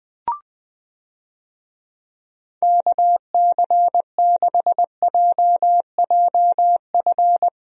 Morse code, heard as a keyed tone, about 20 wpm, 700 Hz. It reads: KC6JJF
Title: USA Call Signs - 6 Characters - Repeated 2x - 20wpm